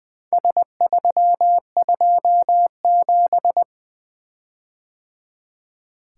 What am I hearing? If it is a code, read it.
Morse code, heard as a keyed tone, about 22 wpm, 700 Hz. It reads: S327
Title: Speed-Racing - Four Letter-Number Combinations - 20wpm